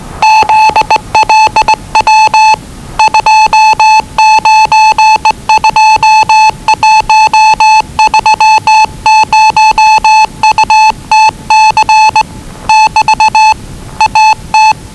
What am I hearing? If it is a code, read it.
Morse code, heard as a keyed tone, about 20 wpm, 870 Hz. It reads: ZLW 292130UTC <BT> AT
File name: WellingtonRadioZlwCloseDownOnHfAndMf / 01-Zlw 4277khz 417.5 qrt until 0925z